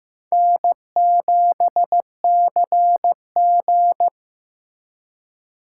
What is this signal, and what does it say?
Morse code, heard as a keyed tone, about 15 wpm, 700 Hz. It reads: N7CG
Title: USA Call Signs - 4 Characters - 3x Character Spacing - 15wpm